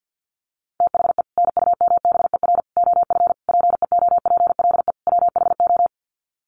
Morse code, heard as a keyed tone, about 48 wpm, 700 Hz. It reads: THE NUMBER OF PEOPLE WHO
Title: Sets of 5 Words Encore 50wpm